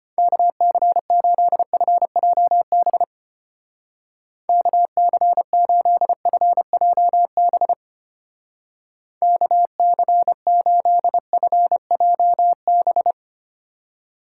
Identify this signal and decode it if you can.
Morse code, heard as a keyed tone, about 31 wpm, 700 Hz. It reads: KC8FJB KC8FJB KC8FJB
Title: Speed Racing - USA 2x3 Call Signs 25wpm